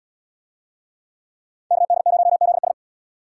Morse code, heard as a keyed tone, about 77 wpm, 700 Hz. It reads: KD0ZS